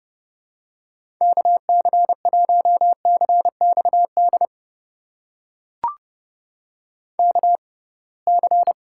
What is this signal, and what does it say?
Morse code, heard as a keyed tone, about 32 wpm, spaced as for 23 wpm, 700 Hz. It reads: KC1CXD KC
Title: USA Call Signs - 6 Characters - 6x Character Spacing - 30wpm